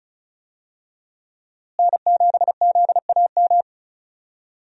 Morse code, heard as a keyed tone, about 36 wpm, 700 Hz. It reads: N7ZAM
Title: USA Call Signs - 5 Characters - Repeated 3x - 35wpm